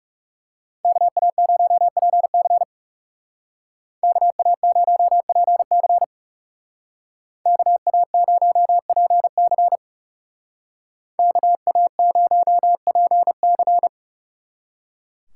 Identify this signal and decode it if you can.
Morse code, heard as a keyed tone, about 35 wpm, 700 Hz. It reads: KA0PC KA0PC KA0PC KA0PC
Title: Speed Racing - USA 2x2 Call Signs 30wpm